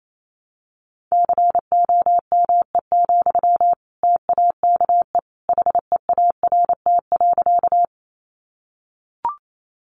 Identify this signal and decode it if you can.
Morse code, heard as a keyed tone, about 28 wpm, 700 Hz. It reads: COME, TAKE HEART.
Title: Sentences from Top 1200 Words 28wpm